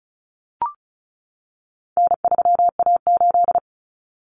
Morse code, heard as a keyed tone, about 34 wpm, 700 Hz. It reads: N3A8
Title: Common Cut Numbers - 4-Digit 35wpm